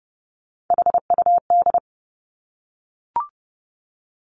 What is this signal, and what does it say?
Morse code, heard as a keyed tone, about 30 wpm, 700 Hz. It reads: HUD
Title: Three Challenging Letter-Number Combinations 30wpm